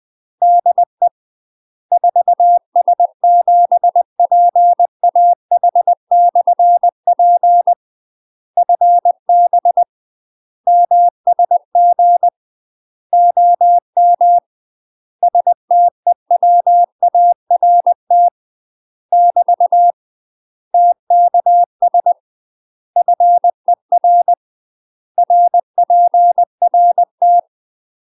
Morse code, heard as a keyed tone, about 20 wpm, 700 Hz. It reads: DE 4S7PAH/P FB MSG OM STEWART <BT> TKS FER RPRT
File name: UK MASTER QSOs TRAINING / 20WPM / UK MASTER QSOs 0024